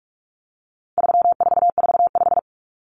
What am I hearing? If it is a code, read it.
Morse code, heard as a keyed tone, about 44 wpm, 700 Hz. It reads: 3445
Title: Numbers - 4-Digit 45wpm